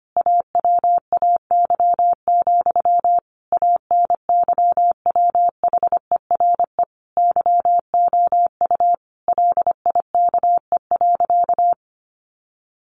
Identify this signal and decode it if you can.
Morse code, heard as a keyed tone, about 25 wpm, 700 Hz. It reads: AWAY, ANYWHERE YOU LIKE.
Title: Sentences from Top 1400 Words 25wpm